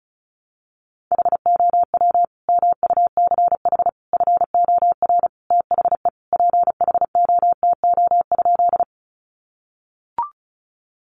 Morse code, heard as a keyed tone, about 34 wpm, 700 Hz. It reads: HOW MUCH FOR THE PHOTO?
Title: Sentences from Top 1200 Words 35wpm